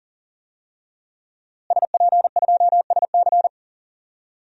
Morse code, heard as a keyed tone, about 38 wpm, 700 Hz. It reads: SP2SC
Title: DXCC-100 Call Signs - 5 Characters - Repeated 3x - 40wpm